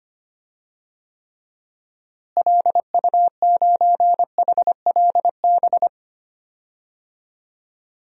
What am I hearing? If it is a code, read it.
Morse code, heard as a keyed tone, about 25 wpm, 700 Hz. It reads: LU9HLB
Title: DXCC-100 Call Signs - 6 Characters - Repeated 2x - 25wpm